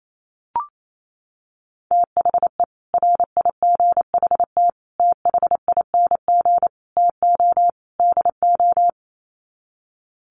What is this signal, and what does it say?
Morse code, heard as a keyed tone, about 28 wpm, 700 Hz. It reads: THE RIGHT THING TO DO
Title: Sets of 5 Words 28wpm